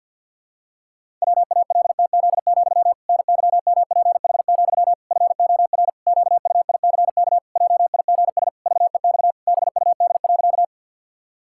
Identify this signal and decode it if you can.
Morse code, heard as a keyed tone, about 47 wpm, 700 Hz. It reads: WALTZ, NYMPH, FOR QUICK JIGS VEX BUD.
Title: Pangram Challenge v2 - 50wpm